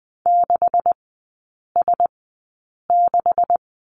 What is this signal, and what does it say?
Morse code, heard as a keyed tone, about 20 wpm, spaced as for 8 wpm, 700 Hz. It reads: 6S6
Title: 152 Lesson 25e - Letter p - Three Characters 20wpm